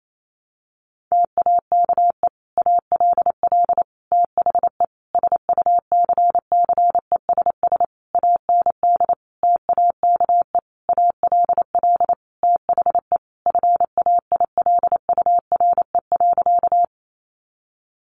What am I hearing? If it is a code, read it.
Morse code, heard as a keyed tone, about 28 wpm, 700 Hz. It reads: TAKE ALL THE SUCCESS AND TAKE ALL THE FAILURE.
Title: Sentences from Top 900 Words 28wpm